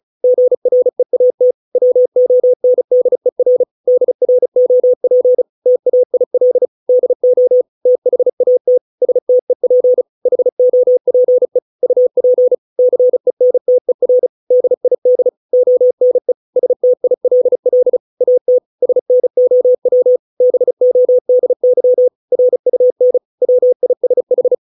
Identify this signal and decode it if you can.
Morse code, heard as a keyed tone, about 34 wpm, 500 Hz. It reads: GREAT WONDER DROP TAIL DO THAT STEP HOPE UP CENTER DID ONE STILL AT SNOW BODY RUN WISH